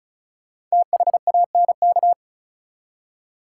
Morse code, heard as a keyed tone, about 36 wpm, 700 Hz. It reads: THANK
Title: Five Letter Words - 35wpm